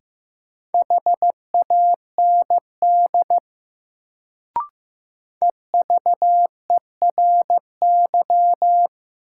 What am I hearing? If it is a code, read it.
Morse code, heard as a keyed tone, about 15 wpm, 700 Hz. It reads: HAND EVERY
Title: Sentences from Top 200 Words - New Word Drill - 15wpm